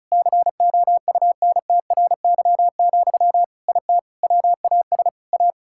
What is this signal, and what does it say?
Morse code, heard as a keyed tone, about 36 wpm, 700 Hz. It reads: COUNTRY, IT WAS A
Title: Sentences from Top 1000 Words - Review Part 5 - 35wpm